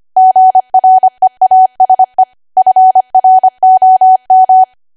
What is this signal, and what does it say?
Morse code, heard as a keyed tone, about 25 wpm, 750 Hz. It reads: GREASE FROM
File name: Archive / 25 WPM / 25WPM